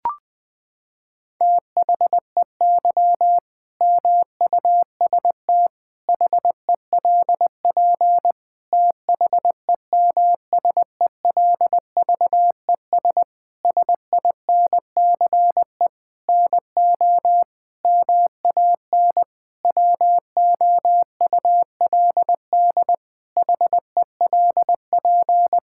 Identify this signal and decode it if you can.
Morse code, heard as a keyed tone, about 20 wpm, 700 Hz. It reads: THEY MUST HELP THEMSELVES SINCE NO MAN WOULD HELP
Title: Sentences from Top 1000 Words - Review Part 5 - 20wpm